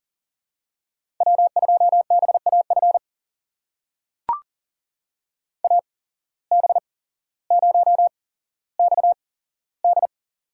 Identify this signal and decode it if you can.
Morse code, heard as a keyed tone, about 38 wpm, spaced as for 17 wpm, 700 Hz. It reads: W2BAF AB0XD
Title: USA Call Signs - 5 Characters - 8x Character Spacing - 40wpm